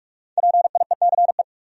Morse code, heard as a keyed tone, about 58 wpm, spaced as for 51 wpm, 700 Hz. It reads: PIECE
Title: Speed-Racing Top 500 Words 45wpm